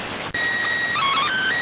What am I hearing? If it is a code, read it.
SSTV (slow-scan television), PD 290.